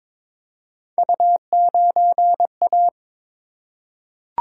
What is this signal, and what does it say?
Morse code, heard as a keyed tone, about 22 wpm, 700 Hz. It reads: U9A
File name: practice / 22wpm / Three Letter-Number Combinations - 22wpm